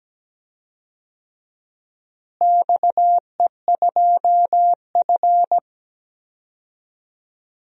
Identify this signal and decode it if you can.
Morse code, heard as a keyed tone, about 17 wpm, 700 Hz. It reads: XE2F
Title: DXCC-100 Call Signs - 4 Characters - Repeated 3x - 17wpm